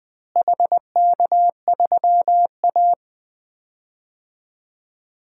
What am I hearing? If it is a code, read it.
Morse code, heard as a keyed tone, about 20 wpm, 700 Hz. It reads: HK3A
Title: DXCC-100 Call Signs - 4 Characters - Repeated 3x - 20wpm